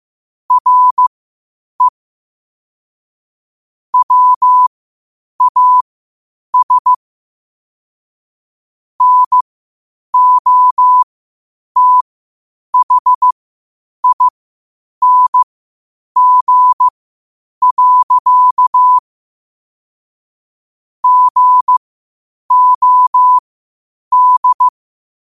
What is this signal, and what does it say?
Morse code, heard as a keyed tone, about 15 wpm, spaced as for 8 wpm, 1000 Hz. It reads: RE WAS NOTHING. GOD
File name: qotdfiles / 8wpm